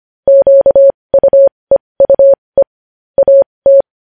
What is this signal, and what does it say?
Morse code, heard as a keyed tone, about 25 wpm, spaced as for 20 wpm, 550 Hz. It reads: QUEUE AT
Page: News headlines in morse code at 20 wpm